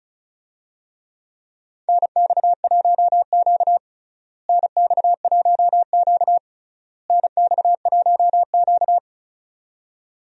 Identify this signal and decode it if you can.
Morse code, heard as a keyed tone, about 36 wpm, 700 Hz. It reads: NX1Q NX1Q NX1Q
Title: USA Call Signs - 4 Characters - Repeated 3x - 35wpm